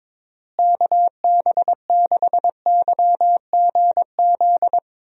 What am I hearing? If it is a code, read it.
Morse code, heard as a keyed tone, about 22 wpm, 700 Hz. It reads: KB6YGZ
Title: USA Call Signs - 6 Characters - Repeated 2x - 22wpm